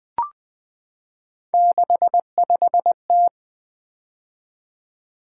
Morse code, heard as a keyed tone, about 20 wpm, 700 Hz. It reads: 65T
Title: Three Letter-Number Combinations - 20wpm